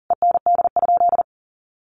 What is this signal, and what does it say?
Morse code, heard as a keyed tone, about 40 wpm, 700 Hz. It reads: END?
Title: Sentences from Top 1200 Words 40wpm